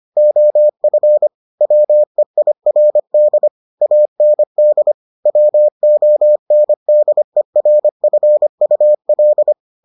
Morse code, heard as a keyed tone, about 25 wpm, 600 Hz. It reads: OF WEIRD AND WONDERFUL